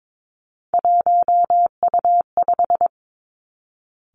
Morse code, heard as a keyed tone, about 22 wpm, 700 Hz. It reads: 1U5
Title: Three Challenging Letter-Number Combinations 22wpm